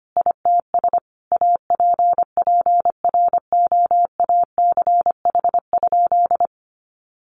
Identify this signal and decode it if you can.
Morse code, heard as a keyed tone, about 25 wpm, 700 Hz. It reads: ITS APPROACH?
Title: Sentences from Top 1500 Words 25wpm